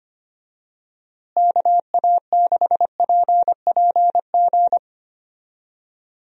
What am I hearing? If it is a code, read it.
Morse code, heard as a keyed tone, about 25 wpm, 700 Hz. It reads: KA6PPG
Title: USA Call Signs - 6 Characters - Repeated 2x - 25wpm